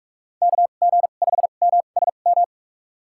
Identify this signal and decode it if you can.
Morse code, heard as a keyed tone, about 47 wpm, spaced as for 33 wpm, 700 Hz. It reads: KG5MSM
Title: USA Call Signs - 2x Character Spacing - 45wpm